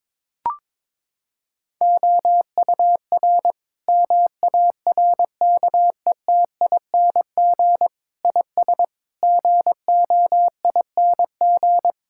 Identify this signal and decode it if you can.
Morse code, heard as a keyed tone, about 22 wpm, 700 Hz. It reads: OUR MARKETING IS GOING
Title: Sentences from Top 1000 Words - Review Part 5 - 22wpm